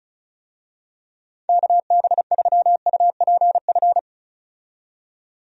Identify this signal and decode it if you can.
Morse code, heard as a keyed tone, about 34 wpm, 700 Hz. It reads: KB3UPF
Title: USA Call Signs - 6 Characters - 35wpm